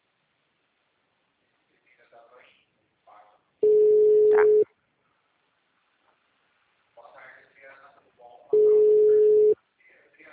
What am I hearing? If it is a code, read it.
Ringback tone (Europe, 425 Hz), with speech.
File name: out- -2004- - - .192